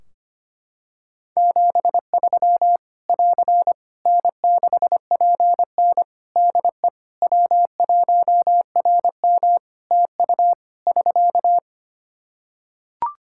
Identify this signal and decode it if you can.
Morse code, heard as a keyed tone, about 25 wpm, 700 Hz. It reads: 73 <AR> N6PN DE W1RM TU <SK>
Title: Practice QSO - Exchange 4 - Ending - 25wpm